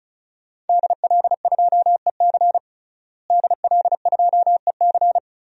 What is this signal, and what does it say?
Morse code, heard as a keyed tone, about 34 wpm, 700 Hz. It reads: DL2EC DL2EC
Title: DXCC-100 Call Signs - 5 Characters - Repeated 3x - 35wpm